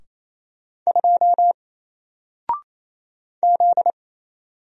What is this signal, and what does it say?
Morse code, heard as a keyed tone, about 27 wpm, 700 Hz. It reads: 2 Z
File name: Single-Letter-Number-28wpm